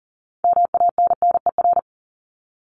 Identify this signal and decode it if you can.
Morse code, heard as a keyed tone, about 40 wpm, 700 Hz. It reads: MANNER
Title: Top 1501-1750 Words 40wpm